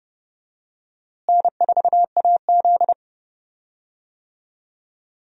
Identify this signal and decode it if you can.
Morse code, heard as a keyed tone, about 30 wpm, 700 Hz. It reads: N4AZ
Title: USA Call Signs - 4 Characters - Repeated 2x - 30wpm